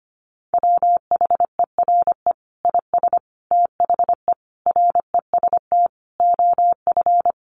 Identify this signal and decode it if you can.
Morse code, heard as a keyed tone, about 25 wpm, 700 Hz. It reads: WHERE IS THE REST OF
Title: Sentences from Top 1500 Words 25wpm